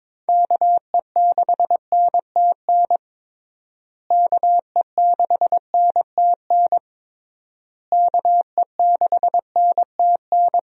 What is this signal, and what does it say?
Morse code, heard as a keyed tone, about 22 wpm, 700 Hz. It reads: KE6NTN KE6NTN KE6NTN
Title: USA Call Signs - 6 Characters - Repeated 3x - 22wpm